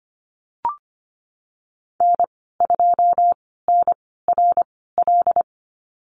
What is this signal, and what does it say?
Morse code, heard as a keyed tone, about 25 wpm, spaced as for 16 wpm, 700 Hz. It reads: N2NRL
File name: Lesson 15g - Number 5 - Call Signs 25wpm